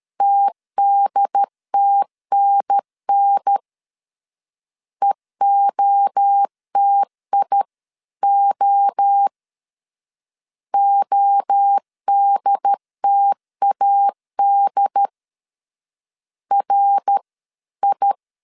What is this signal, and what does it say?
Morse code, heard as a keyed tone, about 13 wpm, 800 Hz. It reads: TDTNN EOTIO ODTAD R I
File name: CW / DET / 1 3 1500C S B 12wpm 2828